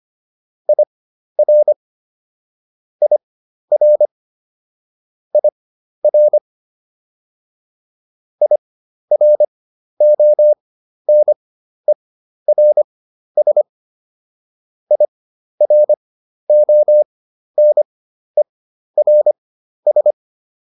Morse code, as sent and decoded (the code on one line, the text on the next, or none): .. .-. / .. .-. / .. .-. / .. .-. --- -. .-. ... / .. .-. --- -. .-. ...
IR IR IR IRONERS IRONERS